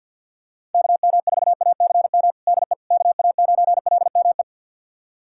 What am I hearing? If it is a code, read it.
Morse code, heard as a keyed tone, about 52 wpm, 700 Hz. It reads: KM4ACM DE KA9LME
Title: USA Callsign DE Callsign 50wpm